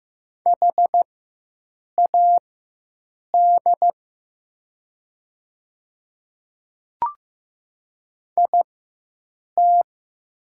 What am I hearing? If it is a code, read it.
Morse code, heard as a keyed tone, about 15 wpm, spaced as for 7 wpm, 700 Hz. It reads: HAD IT
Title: 4x Character Spacing - Top 100 Words - 15wpm